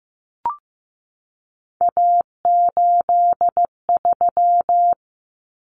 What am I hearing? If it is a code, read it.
Morse code, heard as a keyed tone, about 15 wpm, 700 Hz. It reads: A83
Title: Common Cut Numbers - 3-Digit 15wpm